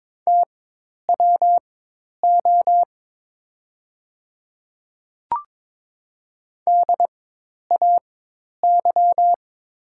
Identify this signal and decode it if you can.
Morse code, heard as a keyed tone, about 22 wpm, spaced as for 10 wpm, 700 Hz. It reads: TWO DAY
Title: 4x Character Spacing - Top 100 Words - 22wpm